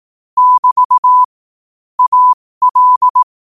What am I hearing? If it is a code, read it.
Morse code, heard as a keyed tone, about 18 wpm, spaced as for 15 wpm, 1000 Hz. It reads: <BT> AL
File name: qotdfiles / 15wpm